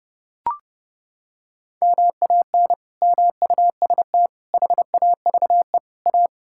Morse code, heard as a keyed tone, about 32 wpm, 700 Hz. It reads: MAN MUST HAVE A
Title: Sentences from Top 1000 Words - Review Part 6 - 30wpm